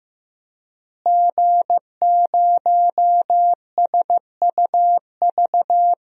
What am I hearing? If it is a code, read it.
Morse code, heard as a keyed tone, about 15 wpm, 700 Hz. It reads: G0SUV
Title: DXCC-100 Call Signs - 5 Characters - Repeated 2x - 15wpm